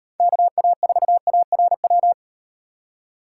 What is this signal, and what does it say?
Morse code, heard as a keyed tone, about 38 wpm, 700 Hz. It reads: KA4ARW